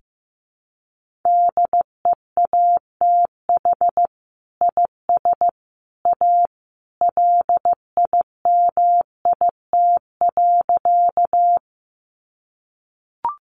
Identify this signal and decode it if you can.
Morse code, heard as a keyed tone, about 15 wpm, 700 Hz. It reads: DEATH IS A LIMIT.
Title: Sentences from Top 1500 Words 15wpm